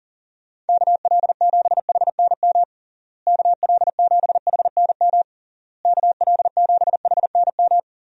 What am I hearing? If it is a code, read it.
Morse code, heard as a keyed tone, about 40 wpm, 700 Hz. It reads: KL7HNM KL7HNM KL7HNM
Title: USA Call Signs - 6 Characters - Repeated 3x - 40wpm